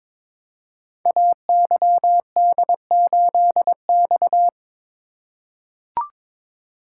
Morse code, heard as a keyed tone, about 22 wpm, 700 Hz. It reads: AYD8X
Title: Code Groups - 5 Letter:Numbers - 22wpm